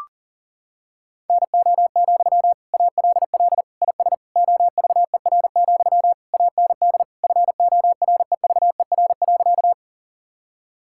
Morse code, heard as a keyed tone, about 40 wpm, 700 Hz. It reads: NO, ALL IS OVER, AND FOREVER.
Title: Sentences from Top 1600 Words 40wpm